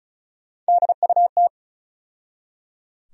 Morse code, heard as a keyed tone, about 36 wpm, 700 Hz. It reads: DUT
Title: USA Call Sign Postfixes - 3 Letters 35wpm